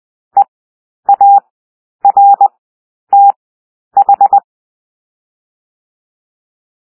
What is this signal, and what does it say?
Morse code, heard as a keyed tone, about 20 wpm, 800 Hz. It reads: E A R T H
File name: cwmp3 / W1 SOUND WORDS 20 wpm